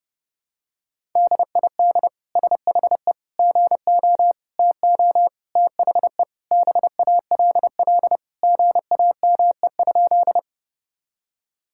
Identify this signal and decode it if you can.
Morse code, heard as a keyed tone, about 30 wpm, 700 Hz. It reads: DID SHE GO TO THE BALL GAME?
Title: Sentences from Top 1000 Words - Review Part 6 - 30wpm